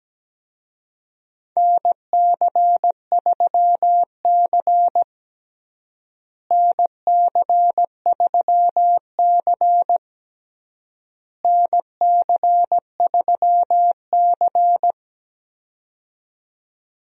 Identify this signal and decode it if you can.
Morse code, heard as a keyed tone, about 17 wpm, 700 Hz. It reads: NC3C NC3C NC3C
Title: USA Call Signs - 4 Characters - Repeated 3x - 17wpm